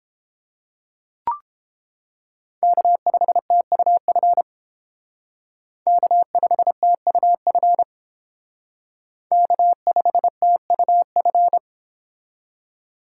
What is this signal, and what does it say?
Morse code, heard as a keyed tone, about 29 wpm, 700 Hz. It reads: K5TUF K5TUF K5TUF